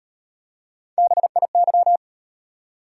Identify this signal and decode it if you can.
Morse code, heard as a keyed tone, about 38 wpm, 700 Hz. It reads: BIY